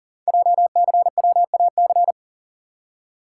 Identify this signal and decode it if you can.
Morse code, heard as a keyed tone, about 40 wpm, 700 Hz. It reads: JCWAC
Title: Code Groups - 5 Letters - 40wpm